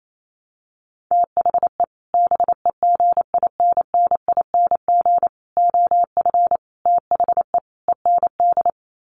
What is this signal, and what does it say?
Morse code, heard as a keyed tone, about 28 wpm, 700 Hz. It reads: THE BEGINNING OF THE END
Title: Sets of 5 Words Encore2 28wpm